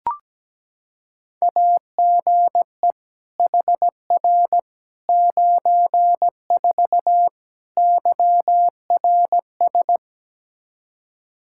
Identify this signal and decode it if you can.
Morse code, heard as a keyed tone, about 17 wpm, 700 Hz. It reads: AGE HR 94 YRS
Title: Practice QSO - Exchange 3 - More Info - 17wpm